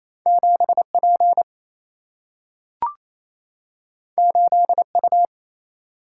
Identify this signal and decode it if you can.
Morse code, heard as a keyed tone, about 28 wpm, 700 Hz. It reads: 7 P 8 U
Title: Single-Letter Single-Number 28wpm